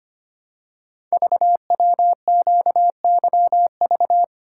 Morse code, heard as a keyed tone, about 25 wpm, 700 Hz. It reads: VWQYV
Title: Code Groups - 5 Letters - 25wpm